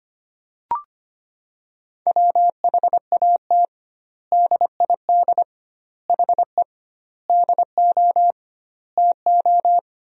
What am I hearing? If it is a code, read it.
Morse code, heard as a keyed tone, about 25 wpm, 700 Hz. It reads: WHAT DID HE DO TO
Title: 2x Word-Spacing - Sentences from Top 100 Words - 25wpm